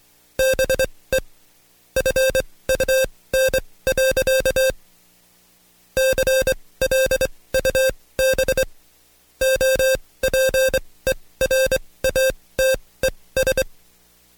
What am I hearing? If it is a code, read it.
Morse code, heard as a keyed tone, about 23 wpm, spaced as for 18 wpm, 530 Hz. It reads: BE FUN. CLUB OPERATES
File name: Code Quick / AudioDownld2 / Seg135-Stepper8 17WPM